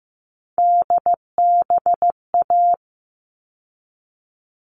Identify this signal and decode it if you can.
Morse code, heard as a keyed tone, about 15 wpm, 700 Hz. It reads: DBA
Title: All Cut Numbers - 3-Digit 15wpm